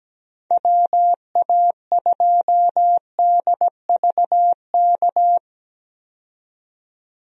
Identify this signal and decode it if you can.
Morse code, heard as a keyed tone, about 17 wpm, 700 Hz. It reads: WA2DVK
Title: USA Call Signs - 6 Characters - 17wpm